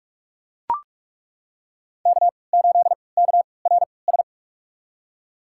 Morse code, heard as a keyed tone, about 42 wpm, spaced as for 25 wpm, 700 Hz. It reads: K8KRS
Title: USA Call Signs - 5 Characters - 3x Character Spacing - 45wpm